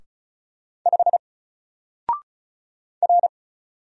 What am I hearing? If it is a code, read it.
Morse code, heard as a keyed tone, about 35 wpm, 700 Hz. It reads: 5 R
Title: Single Letter-Number 35wpm